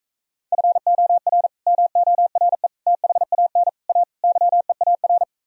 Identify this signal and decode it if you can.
Morse code, heard as a keyed tone, about 45 wpm, 700 Hz. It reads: FOR MORE THAN A YEAR